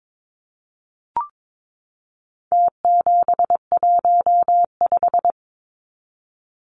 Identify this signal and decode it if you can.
Morse code, heard as a keyed tone, about 22 wpm, 700 Hz. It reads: T715
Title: Numbers - Contest Style - 4-Digit 22wpm